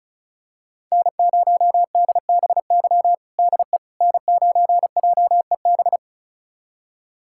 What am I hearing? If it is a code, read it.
Morse code, heard as a keyed tone, about 36 wpm, 700 Hz. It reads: N0DBY DE N9JEB
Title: USA Callsign DE Callsign 35wpm